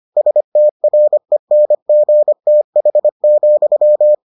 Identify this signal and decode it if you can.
Morse code, heard as a keyed tone, about 25 wpm, 600 Hz. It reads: STRENGTH,